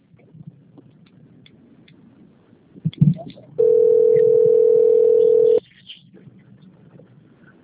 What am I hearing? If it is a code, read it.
Ringback tone (North America), 440 + 480 Hz.
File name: out- - - - - .56634